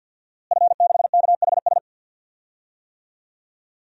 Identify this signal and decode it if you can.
Morse code, heard as a keyed tone, about 53 wpm, 700 Hz. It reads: F6KHS
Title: DXCC-100 Call Signs - 5 Characters - Repeated 2x - 50wpm